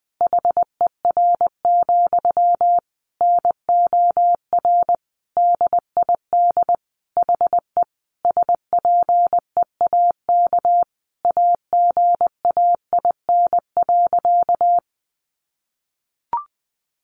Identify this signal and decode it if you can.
Morse code, heard as a keyed tone, about 19 wpm, 700 Hz. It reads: HER, NOR DID HE SPEAK AGAIN.
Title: Sentences from Top 1500 Words 20wpm